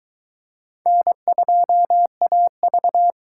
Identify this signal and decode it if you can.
Morse code, heard as a keyed tone, about 23 wpm, 700 Hz. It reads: N2AV